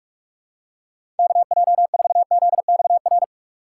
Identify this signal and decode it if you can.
Morse code, heard as a keyed tone, about 46 wpm, 700 Hz. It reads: KJ4ZXR
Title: USA Call Signs - 6 Characters - 4x Character Spacing - 45wpm